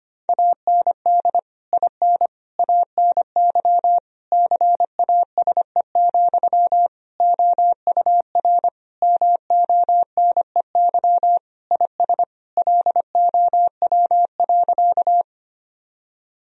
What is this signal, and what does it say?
Morse code, heard as a keyed tone, about 25 wpm, 700 Hz. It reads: AND IN ANY CASE, OUR MONEY IS LOW.
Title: Sentences from Top 1000 Words - Review Part 1 - 25wpm